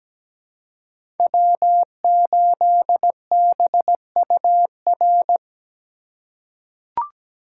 Morse code, heard as a keyed tone, about 17 wpm, 700 Hz. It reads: W8BUR
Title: USA Call Signs - 5 Characters - 3x Character Spacing - 17wpm